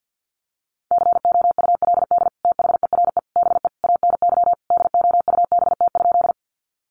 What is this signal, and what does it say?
Morse code, heard as a keyed tone, about 50 wpm, 700 Hz. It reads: COULD THERE BE ANY DOUBT?
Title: Sentences from Top 1500 Words 50wpm